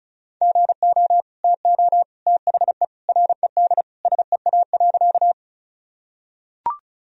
Morse code, heard as a keyed tone, about 34 wpm, 700 Hz. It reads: GO TO THE RED SEA.
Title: Sentences from Top 1000 Words - Review Part 2 - 35wpm